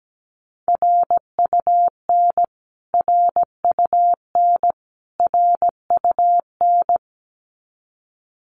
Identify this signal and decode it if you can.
Morse code, heard as a keyed tone, about 17 wpm, 700 Hz. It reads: RUN RUN RUN
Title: Top 300 Words Repeated 17wpm